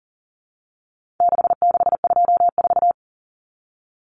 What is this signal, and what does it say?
Morse code, heard as a keyed tone, about 40 wpm, 700 Hz. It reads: 6624
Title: Numbers - 4-Digit 40wpm